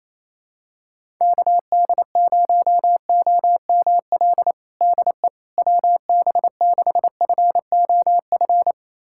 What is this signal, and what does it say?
Morse code, heard as a keyed tone, about 28 wpm, 700 Hz. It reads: KD0OML DE WB6FOF
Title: USA Callsign DE Callsign 28wpm